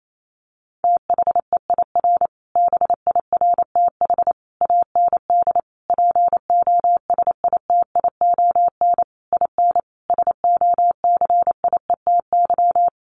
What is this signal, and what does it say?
Morse code, heard as a keyed tone, about 28 wpm, 700 Hz. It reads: THEIR BIRTH AND POSITION IN SOCIETY
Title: Sentences from Top 900 Words 28wpm